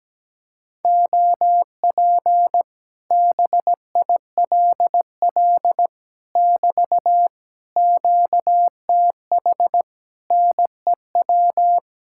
Morse code, heard as a keyed tone, about 17 wpm, 700 Hz. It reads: OP BILL <BT> QTH NEW
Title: Speed-Racing Common QSO Phrases Encore - 17wpm